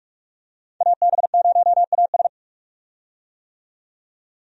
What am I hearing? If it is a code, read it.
Morse code, heard as a keyed tone, about 46 wpm, 700 Hz. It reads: AB0AS
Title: USA Call Signs - 5 Characters - 45wpm